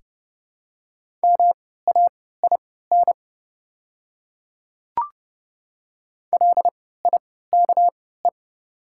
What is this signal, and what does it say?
Morse code, heard as a keyed tone, about 31 wpm, spaced as for 17 wpm, 700 Hz. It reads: MAIN LIKE
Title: 3x Character Spacing - Top 300 Words - 30wpm